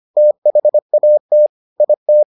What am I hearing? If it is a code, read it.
Morse code, heard as a keyed tone, about 25 wpm, 600 Hz. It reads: THAT IT